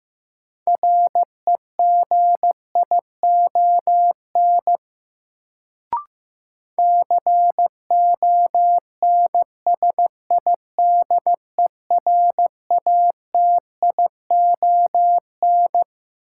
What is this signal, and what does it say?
Morse code, heard as a keyed tone, about 15 wpm, 700 Hz. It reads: REGION CONSIDERATION
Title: Sentences from Top 1500 Words - New Word Drill - 15wpm